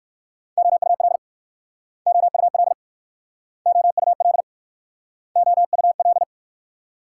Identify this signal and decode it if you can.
Morse code, heard as a keyed tone, about 54 wpm, 690 Hz. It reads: OUL OUL OUL OUL
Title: Speed-Racing Multi-letter Phonemes - 45wpm